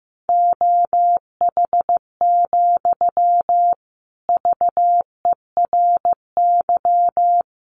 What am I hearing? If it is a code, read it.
Morse code, heard as a keyed tone, about 15 wpm, 700 Hz. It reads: OH, VERY
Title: Sentences from Top 900 Words 15wpm